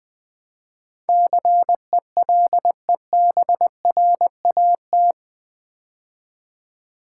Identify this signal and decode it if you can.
Morse code, heard as a keyed tone, about 20 wpm, 700 Hz. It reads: CELEBRAT
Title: Build A Word - 4 Letters At A Time - 11-Letter Words - 20wpm